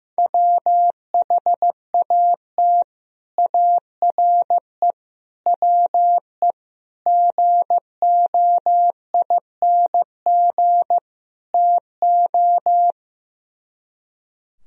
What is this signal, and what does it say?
Morse code, heard as a keyed tone, about 15 wpm, 700 Hz. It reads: WHAT ARE WE GOING TO
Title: Speed-Racing - Sets of 5 Words 15wpm